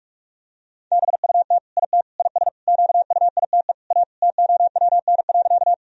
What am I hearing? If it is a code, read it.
Morse code, heard as a keyed tone, about 46 wpm, 700 Hz. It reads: BUT IT IS QUITE A TOWN.
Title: Sentences from Top 1000 Words - Review Part 2 - 45wpm